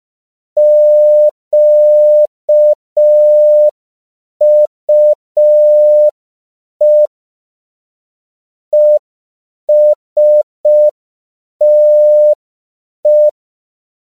Morse code, heard as a keyed tone, about 5 wpm, 600 Hz. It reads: QUE ESTE
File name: CW / 5PPM